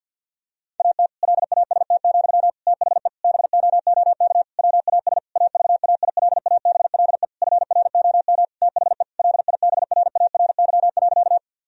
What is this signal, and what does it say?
Morse code, heard as a keyed tone, about 50 wpm, 700 Hz. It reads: AT LAST, THE BOOK WAS AVAILABLE FROM THE LIBRARY.
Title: Sentences from Top 1000 Words - Review Part 3 - 50wpm